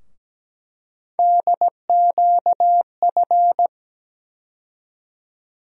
Morse code, heard as a keyed tone, about 17 wpm, 700 Hz. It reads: DQF
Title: USA Call Sign Postfixes - 3 Letters 17wpm